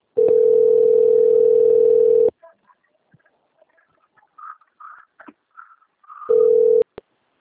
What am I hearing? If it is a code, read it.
Ringback tone (North America), 440 + 480 Hz.